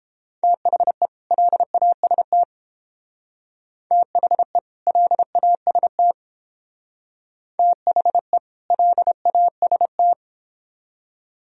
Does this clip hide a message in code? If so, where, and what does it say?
Yes: Morse code from 0.4 to 10.1 s, heard as a keyed tone, about 30 wpm, 700 Hz. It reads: THE LAST THE LAST THE LAST